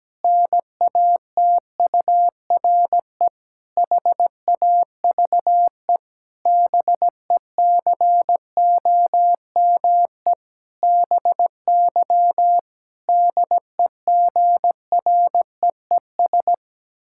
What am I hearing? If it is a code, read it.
Morse code, heard as a keyed tone, about 17 wpm, 700 Hz. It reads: NATURE HAVE BECOME BY DEGREES
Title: Pangram Challenge v2 - 17wpm